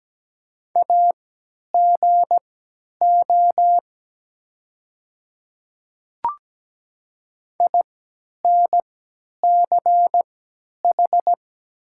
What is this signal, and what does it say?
Morse code, heard as a keyed tone, about 17 wpm, spaced as for 10 wpm, 700 Hz. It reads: AGO INCH
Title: 3x Character Spacing - Top 500 Words - 17wpm